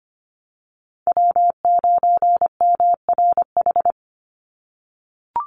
Morse code, heard as a keyed tone, about 25 wpm, 700 Hz. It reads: W9MRH
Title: NA SOTA Chaser Callsigns 25wpm